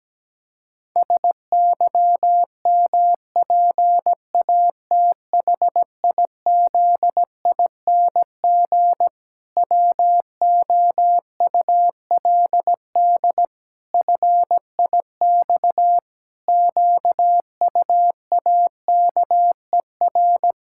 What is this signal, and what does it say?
Morse code, heard as a keyed tone, about 17 wpm, 700 Hz. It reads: SYMPATHIZING WOULD FIX QUAKER
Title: Pangram Challenge v2 - 17wpm